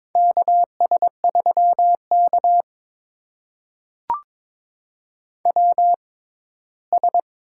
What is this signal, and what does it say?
Morse code, heard as a keyed tone, about 22 wpm, spaced as for 15 wpm, 700 Hz. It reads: KS3K WS
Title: USA Call Signs - 4 Characters - 6x Character Spacing - 22wpm